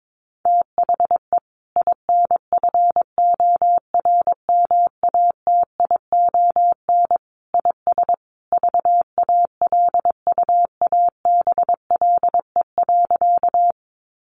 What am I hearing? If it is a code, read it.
Morse code, heard as a keyed tone, about 22 wpm, 700 Hz. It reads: THE INFORMATION IS VALUABLE.
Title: Sentences from Top 1100 Words 22wpm